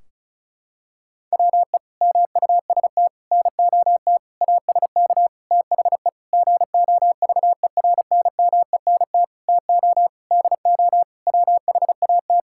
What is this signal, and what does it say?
Morse code, heard as a keyed tone, about 34 wpm, 700 Hz. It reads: WE MUST NOT ASK THE GOVERNMENT TO DO WHAT
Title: Sentences from Top 1600 Words 35wpm